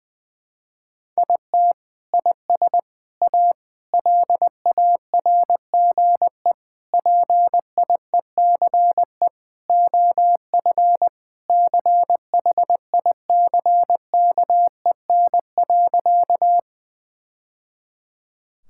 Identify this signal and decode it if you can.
Morse code, heard as a keyed tone, about 21 wpm, 700 Hz. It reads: IT IS A LARGE PIECE OF CHICKEN.
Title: Sentences from Top 1000 Words - Review Part 5 - 20wpm